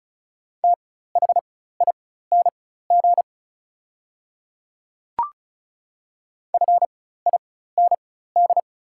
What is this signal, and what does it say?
Morse code, heard as a keyed tone, about 34 wpm, spaced as for 16 wpm, 690 Hz. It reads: THING FIND
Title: 4x Character Spacing - Top 100 Words - 35wpm